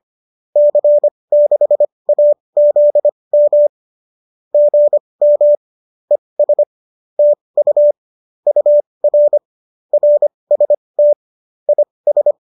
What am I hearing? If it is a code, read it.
Morse code, heard as a keyed tone, about 25 wpm, spaced as for 20 wpm, 590 Hz. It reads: C6AZM GM ES TU UR RST IS